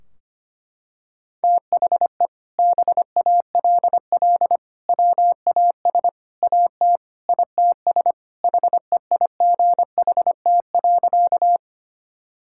Morse code, heard as a keyed tone, about 25 wpm, 700 Hz. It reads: THE BALL WAS AT ITS HEIGHT.
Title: Sentences from Top 1600 Words 25wpm